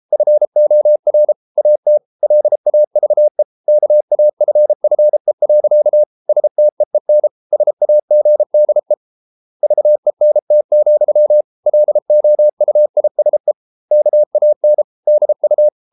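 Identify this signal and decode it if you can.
Morse code, heard as a keyed tone, about 33 wpm, 600 Hz. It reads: FOR AT LAVE KAFFE. STEEN SAGDE VENT, LOUISE KAN DU
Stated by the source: Morse code at 33 wpm, QRQ.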